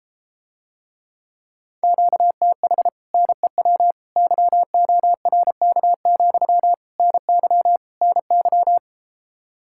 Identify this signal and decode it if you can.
Morse code, heard as a keyed tone, about 33 wpm, 700 Hz. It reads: QTH NEW YORK, NY NY